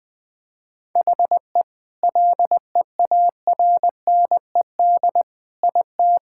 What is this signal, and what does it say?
Morse code, heard as a keyed tone, about 21 wpm, 700 Hz. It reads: HE LEARNED IT
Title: Sentences with 7-Word Phrases - 20wpm